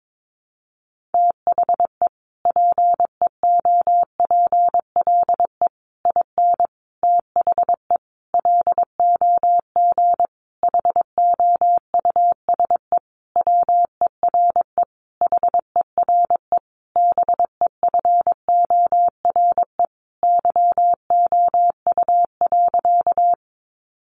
Morse code, heard as a keyed tone, about 22 wpm, 700 Hz. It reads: THE PEOPLE IN THE LOG HOUSE WERE HERE BEFORE YOU.
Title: Sentences from Top 1400 Words 22wpm